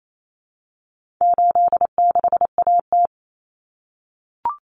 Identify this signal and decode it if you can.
Morse code, heard as a keyed tone, about 28 wpm, 700 Hz. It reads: 86AT
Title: Common Cut Numbers - 4-Digit 28wpm